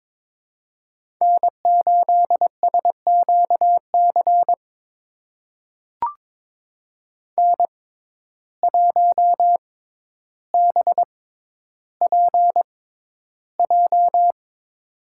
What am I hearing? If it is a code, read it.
Morse code, heard as a keyed tone, about 22 wpm, spaced as for 11 wpm, 700 Hz. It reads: N8SQC N1BPJ
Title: USA Call Signs - 5 Characters - 6x Character Spacing - 22wpm